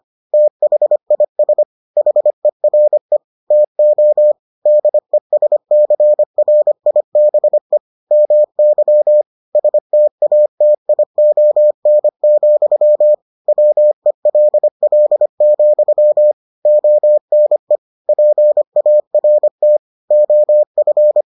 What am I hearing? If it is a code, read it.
Morse code, heard as a keyed tone, about 25 wpm, 600 Hz. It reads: THIS HERE TO DESCRIBE MY STATION, WELL, ONE PART OF